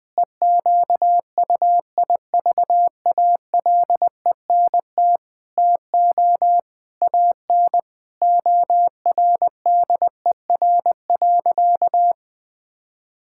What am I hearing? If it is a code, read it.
Morse code, heard as a keyed tone, about 20 wpm, 700 Hz. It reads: EQUIVALENT TO AN ORDER.
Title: Sentences from Top 1600 Words 20wpm